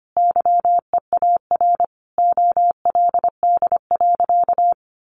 Morse code, heard as a keyed tone, about 25 wpm, 700 Hz. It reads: YEAR OLD.
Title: Sentences from Top 1400 Words 25wpm